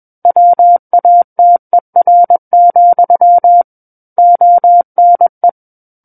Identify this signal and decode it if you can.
Morse code, heard as a keyed tone, about 21 wpm, 700 Hz. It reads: WATER, ONE
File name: Book One / 12-Chapter 12 - 21 WPM